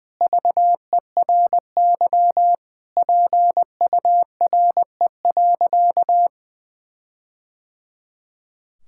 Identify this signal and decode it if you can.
Morse code, heard as a keyed tone, about 21 wpm, 700 Hz. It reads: VERY PURE.
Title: Sentences from Top 1600 Words 20wpm